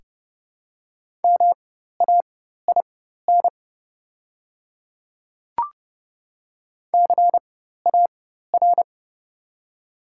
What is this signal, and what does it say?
Morse code, heard as a keyed tone, about 31 wpm, spaced as for 14 wpm, 700 Hz. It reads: MAIN CAR
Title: 4x Character Spacing - Top 300 Words - 30wpm